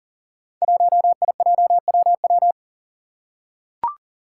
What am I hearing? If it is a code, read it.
Morse code, heard as a keyed tone, about 40 wpm, 700 Hz. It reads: 1IJWW
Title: Five Challenging Letter-Number Combinations 40wpm